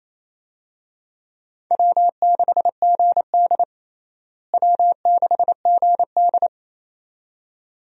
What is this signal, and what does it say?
Morse code, heard as a keyed tone, about 28 wpm, 700 Hz. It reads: W6GD W6GD
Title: USA Call Signs - 4 Characters - Repeated 2x - 28wpm